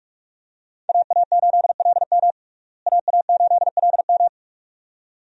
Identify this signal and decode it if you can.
Morse code, heard as a keyed tone, about 46 wpm, 700 Hz. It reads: AA8LM AA8LM
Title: USA Call Signs - 5 Characters - Repeated 2x - 45wpm